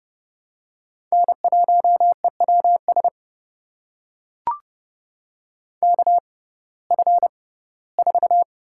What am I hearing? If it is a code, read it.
Morse code, heard as a keyed tone, about 30 wpm, spaced as for 18 wpm, 700 Hz. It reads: N1EWS KF4
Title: USA Call Signs - 5 Characters - 6x Character Spacing - 30wpm